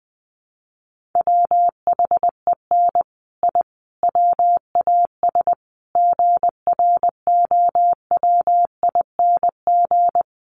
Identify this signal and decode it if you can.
Morse code, heard as a keyed tone, about 20 wpm, 700 Hz. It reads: WHEN I WAS GROWING
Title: Sets of 5 Words Encore 20wpm